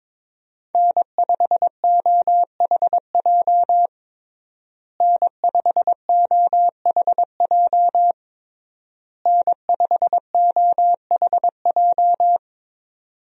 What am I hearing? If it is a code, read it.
Morse code, heard as a keyed tone, about 22 wpm, 700 Hz. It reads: N5OHJ N5OHJ N5OHJ
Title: USA Call Signs - 5 Characters - Repeated 3x - 22wpm